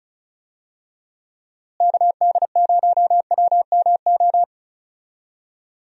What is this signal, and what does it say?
Morse code, heard as a keyed tone, about 36 wpm, 700 Hz. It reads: KD0WMO
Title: USA Call Signs - 6 Characters - Repeated 3x - 35wpm